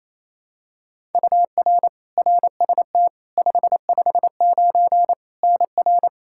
Morse code, heard as a keyed tone, about 28 wpm, 700 Hz. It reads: UR RST 559 NR
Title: Practice QSO - Exchange 1 - Beginning - 28wpm